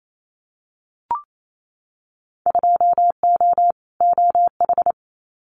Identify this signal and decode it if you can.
Morse code, heard as a keyed tone, about 28 wpm, 700 Hz. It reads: 2O OH
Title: Winter Field Day Exchanges 28wpm